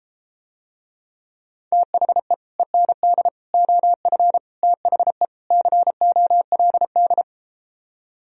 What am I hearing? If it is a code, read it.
Morse code, heard as a keyed tone, about 32 wpm, 700 Hz. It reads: THE END OF THE COLD